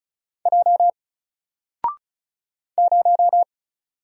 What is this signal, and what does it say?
Morse code, heard as a keyed tone, about 35 wpm, 700 Hz. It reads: J 0
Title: Single Letter-Number Alt 35wpm